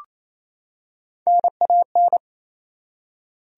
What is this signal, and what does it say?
Morse code, heard as a keyed tone, about 28 wpm, 700 Hz. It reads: NAN
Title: USA Common Names - Max 3 Letters - 28wpm